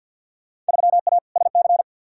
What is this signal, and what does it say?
Morse code, heard as a keyed tone, about 52 wpm, 700 Hz. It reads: 3A SC
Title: Field Day Exchanges - 50wpm